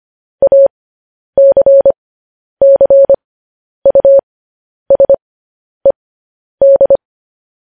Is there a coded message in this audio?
Yes: Morse code from 0.4 to 7.0 s, heard as a keyed tone, about 25 wpm, spaced as for 10 wpm, 550 Hz. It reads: ACCUSED